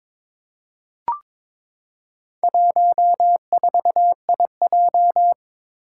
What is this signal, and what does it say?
Morse code, heard as a keyed tone, about 22 wpm, 700 Hz. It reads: 14IJ
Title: Four Challenging Letter-Number Combinations 22wpm